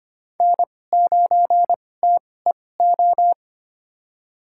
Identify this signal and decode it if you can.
Morse code, heard as a keyed tone, about 25 wpm, spaced as for 18 wpm, 700 Hz. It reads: N9TEO
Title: USA Call Signs - 5 Characters - 2x Character Spacing - 25wpm